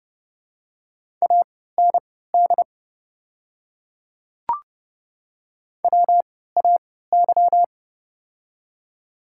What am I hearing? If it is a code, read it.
Morse code, heard as a keyed tone, about 31 wpm, spaced as for 17 wpm, 700 Hz. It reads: AND WAY
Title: 3x Character Spacing - Top 100 Words - 30wpm